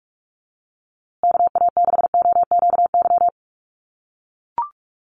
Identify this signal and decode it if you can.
Morse code, heard as a keyed tone, about 44 wpm, 700 Hz. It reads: KA6OQY
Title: USA Call Signs Repeated 2x 45wpm